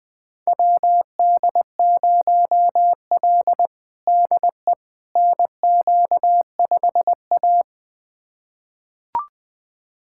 Morse code, heard as a keyed tone, about 20 wpm, 700 Hz. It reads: WD0L DE NQ5A
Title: USA Callsign DE Callsign - 4 Characters - 20wpm